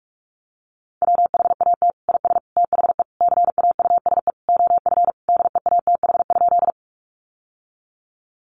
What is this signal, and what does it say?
Morse code, heard as a keyed tone, about 44 wpm, 700 Hz. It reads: WHAT IS THE CAUSE OF DEATH?
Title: Sentences from Top 1100 Words 45wpm